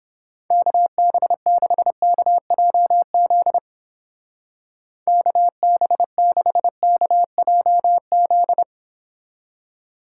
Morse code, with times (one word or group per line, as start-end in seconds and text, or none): 0.5-3.6 KB6KJZ
5.1-8.6 KB6KJZ